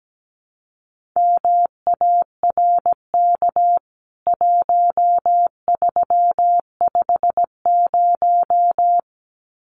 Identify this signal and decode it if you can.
Morse code, heard as a keyed tone, about 17 wpm, 700 Hz. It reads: MARK 1350
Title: CWT Exchange 17wpm